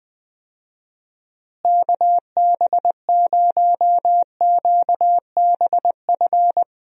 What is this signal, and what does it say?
Morse code, heard as a keyed tone, about 21 wpm, 700 Hz. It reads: KB0QBF
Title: USA Call Signs - 6 Characters - Repeated 2x - 20wpm